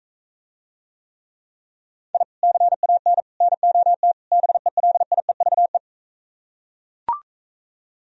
Morse code, heard as a keyed tone, about 42 wpm, 700 Hz. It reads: I CAN NOT BELIEVE